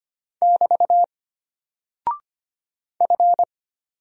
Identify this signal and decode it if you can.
Morse code, heard as a keyed tone, about 25 wpm, 700 Hz. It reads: <BT> F
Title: Single Character (Letter:Number:Punctuation:Prosign) Alt 25wpm